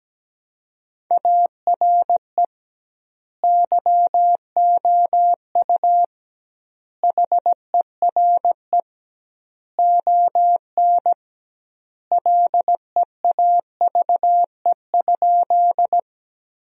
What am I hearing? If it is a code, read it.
Morse code, heard as a keyed tone, about 17 wpm, 700 Hz. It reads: ARE YOU HERE ON LEAVE?
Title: 2x Word-Spacing - Sentences from Top 400 Words - 17wpm